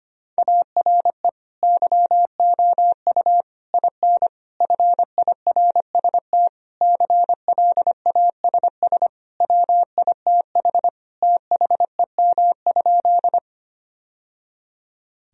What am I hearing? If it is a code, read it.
Morse code, heard as a keyed tone, about 25 wpm, 700 Hz. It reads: ARE YOU IN FIRST CLASS WITH THEM?
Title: Sentences from Top 1000 Words - Review Part 2 - 25wpm